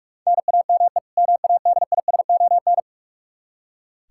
Morse code, heard as a keyed tone, about 46 wpm, 700 Hz. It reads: NAME MADISON
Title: Practice QSO - Exchange 1 - Beginning - 45wpm